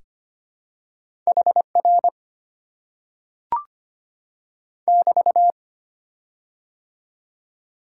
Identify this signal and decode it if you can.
Morse code, heard as a keyed tone, about 25 wpm, 700 Hz. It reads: HR <BT>
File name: Top 90 QSO Elements - 6x Character Spacing - 25wpm